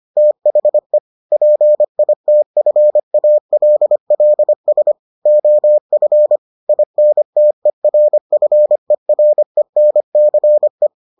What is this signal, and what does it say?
Morse code, heard as a keyed tone, about 25 wpm, 600 Hz. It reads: THE PITFALLS OF INTERFERENCE